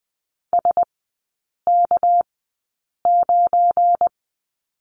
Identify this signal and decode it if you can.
Morse code, heard as a keyed tone, about 20 wpm, spaced as for 8 wpm, 700 Hz. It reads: SK9
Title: 198 Lesson 32e - Letter k - Three Characters 20wpm